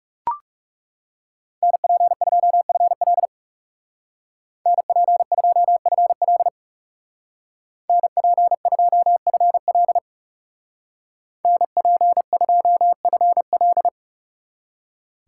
Morse code, heard as a keyed tone, about 37 wpm, 700 Hz. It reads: NP2FL NP2FL NP2FL NP2FL
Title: Speed Racing - USA Call Signs 30wpm